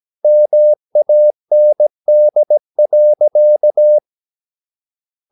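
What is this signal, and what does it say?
Morse code, heard as a keyed tone, about 17 wpm, 600 Hz. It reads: MAND.
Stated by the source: Slow (QRS) Morse practice at 17 wpm, from CW med Gnister.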